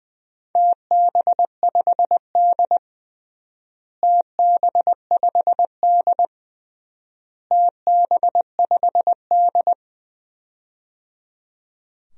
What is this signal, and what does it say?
Morse code, heard as a keyed tone, about 20 wpm, 700 Hz. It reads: TB5D TB5D TB5D
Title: DXCC-100 Call Signs - 4 Characters - Repeated 3x - 20wpm